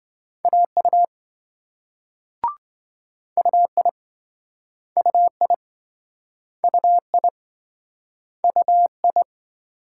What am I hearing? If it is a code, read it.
Morse code, heard as a keyed tone, about 26 wpm, 700 Hz. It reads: AU UI UI UI UI